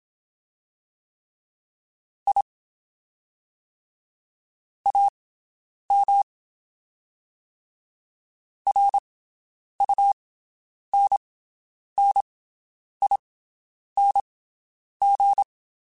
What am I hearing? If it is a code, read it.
Morse code, heard as a keyed tone, about 27 wpm, spaced as for 9 wpm, 800 Hz. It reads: I AM RUNNING
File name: (spaced) / QSOs Spaced-mp3s / WD0BGP 1 @25 sp